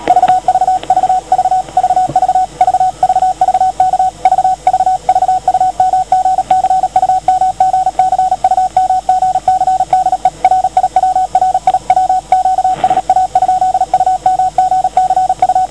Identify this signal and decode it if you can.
Morse code, heard as a keyed tone, about 37 wpm, 750 Hz. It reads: VVV VVV VVV K VVV UMGC UMGC UMGC DE RIW RIW QSA? UMGC U